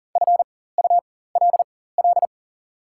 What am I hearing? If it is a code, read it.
Morse code, heard as a keyed tone, about 39 wpm, spaced as for 18 wpm, 690 Hz. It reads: FULL
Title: 4x Character Spacing - Top 500 Words - 40wpm